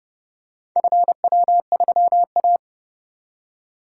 Morse code, heard as a keyed tone, about 29 wpm, 700 Hz. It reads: FW3A